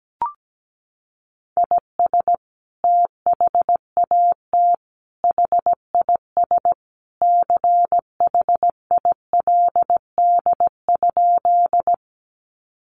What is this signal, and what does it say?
Morse code, heard as a keyed tone, about 17 wpm, 700 Hz. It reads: IS THAT HIS CHILD?
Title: Sentences from Top 800 Words 17wpm